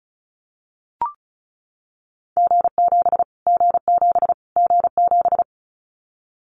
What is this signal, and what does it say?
Morse code, heard as a keyed tone, about 34 wpm, 700 Hz. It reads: G7 G7 G7
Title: Callsign Prefixes for Top 25 DXCC - Repeated 3x - 35wpm